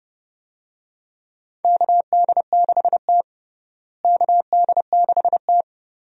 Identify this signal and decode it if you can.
Morse code, heard as a keyed tone, about 32 wpm, 700 Hz. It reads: KD6T KD6T
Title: USA Call Signs - 4 Characters - Repeated 2x - 30wpm